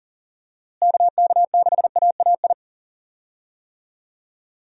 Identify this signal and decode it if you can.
Morse code, heard as a keyed tone, about 37 wpm, 700 Hz. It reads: KK6AAI
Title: USA Call Signs - 6 Characters - Repeated 3x - 40wpm